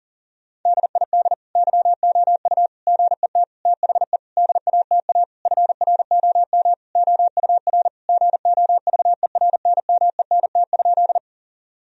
Morse code, heard as a keyed tone, about 40 wpm, 700 Hz. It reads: DID YOU GET THE DATA FROM OUR GOVERNMENT?
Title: Sentences from Top 1000 Words - Review Part 3 - 40wpm